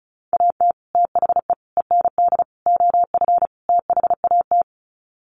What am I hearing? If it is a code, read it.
Morse code, heard as a keyed tone, about 34 wpm, 700 Hz. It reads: AT THE END OF THAT
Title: Sets of 5 Words Encore2 35wpm